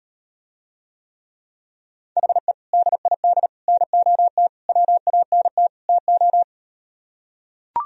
Morse code, heard as a keyed tone, about 38 wpm, 700 Hz. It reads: HE DID NOT WANT TO